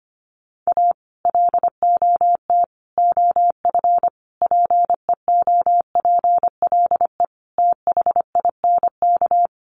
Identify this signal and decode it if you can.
Morse code, heard as a keyed tone, about 25 wpm, 700 Hz. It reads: A LOT OF PEOPLE THINK
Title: Sets of 5 Words Encore 25wpm